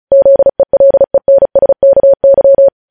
Morse code, heard as a keyed tone, about 36 wpm, 550 Hz. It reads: ZELENSKY